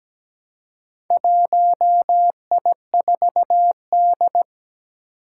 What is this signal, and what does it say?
Morse code, heard as a keyed tone, about 17 wpm, 700 Hz. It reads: 1I4D
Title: Four Challenging Letter-Number Combinations 17wpm